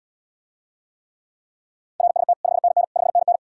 Morse code, heard as a keyed tone, about 80 wpm, 700 Hz. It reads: RST 5NN 5NN